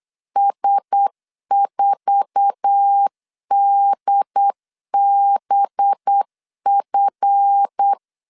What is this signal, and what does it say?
Morse code, heard as a keyed tone, about 8 wpm, 800 Hz. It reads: S4DBF
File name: LEZIONI / 8 YJ45 / 8 YJ45 8wpm 3214